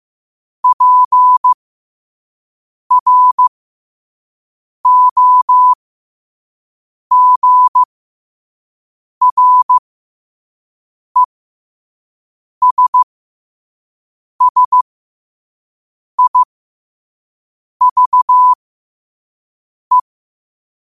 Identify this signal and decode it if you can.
Morse code, heard as a keyed tone, about 15 wpm, spaced as for 5 wpm, 1000 Hz. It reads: PROGRESSIVE
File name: qotdfiles / 5wpm